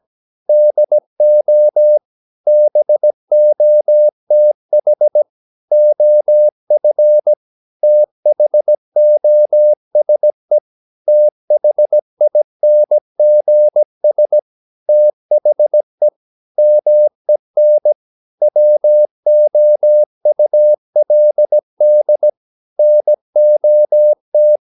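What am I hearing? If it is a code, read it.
Morse code, heard as a keyed tone, about 17 wpm, 600 Hz. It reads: DO BOTH OF THOSE THINGS THE MEN WOULD NOT
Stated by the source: 17 wpm Morse practice sending the English text of a novel.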